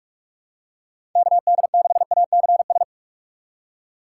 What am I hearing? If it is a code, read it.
Morse code, heard as a keyed tone, about 44 wpm, 700 Hz. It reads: KD6ACS
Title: USA Call Signs - 6 Characters - 6x Character Spacing - 45wpm